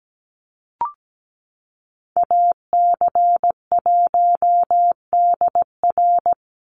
Morse code, heard as a keyed tone, about 17 wpm, 700 Hz. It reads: AC1DR
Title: USA Call Signs 17wpm